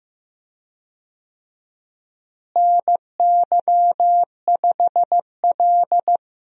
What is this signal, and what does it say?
Morse code, heard as a keyed tone, about 15 wpm, 700 Hz. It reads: NY5L
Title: USA Call Signs - 4 Characters - Repeated 2x - 15wpm